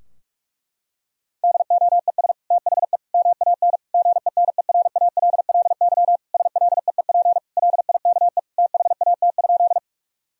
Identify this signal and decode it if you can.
Morse code, heard as a keyed tone, about 41 wpm, 700 Hz. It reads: DOES THE MAN GENERALLY SLEEP LIKE THAT?
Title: Sentences from Top 1000 Words - Review Part 4 - 45wpm